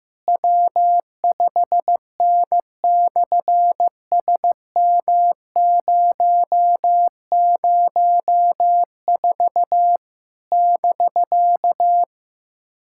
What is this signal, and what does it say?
Morse code, heard as a keyed tone, about 15 wpm, 700 Hz. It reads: W5N/SM004 <BK>
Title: SOTA QSO Practice 15wpm